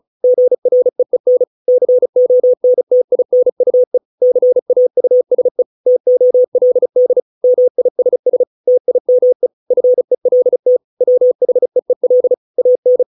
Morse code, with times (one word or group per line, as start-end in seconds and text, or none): 0.2-1.4 GREEN
1.7-4.0 CONTINUE
4.2-5.6 CAUSE
5.9-7.2 TOLD
7.4-8.4 MISS
8.7-9.5 TIME
9.7-10.8 FELT
11.0-12.3 WHEEL
12.6-13.0 AN